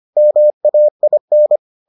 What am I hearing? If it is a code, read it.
Morse code, heard as a keyed tone, about 25 wpm, 600 Hz. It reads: MAIN